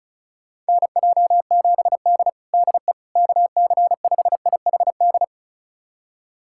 Morse code, heard as a keyed tone, about 34 wpm, 700 Hz. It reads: NJ7D DE KC5IHD
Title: USA Callsign DE Callsign 35wpm